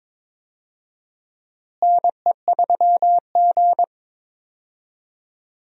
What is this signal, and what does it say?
Morse code, heard as a keyed tone, about 21 wpm, 700 Hz. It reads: NE3G
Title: USA Call Signs - 4 Characters - Repeated 3x - 22wpm